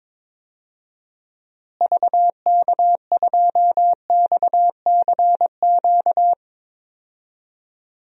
Morse code, heard as a keyed tone, about 22 wpm, 700 Hz. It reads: VK2XCQ
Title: DXCC-100 Call Signs - 6 Characters - Repeated 3x - 22wpm